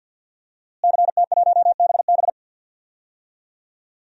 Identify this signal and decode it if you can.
Morse code, heard as a keyed tone, about 50 wpm, 700 Hz. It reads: CT1BB
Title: DXCC-100 Call Signs - 5 Characters - Repeated 2x - 50wpm